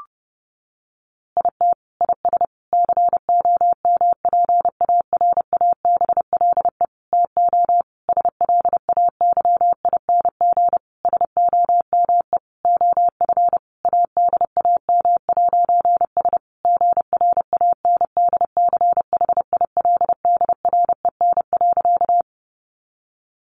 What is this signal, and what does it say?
Morse code, heard as a keyed tone, about 30 wpm, 700 Hz. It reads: IT IS COMPARABLE TO SLAYING SOME OF ADAM'S GRANDCHILDREN.
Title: A Sound Of Thunder 30wpm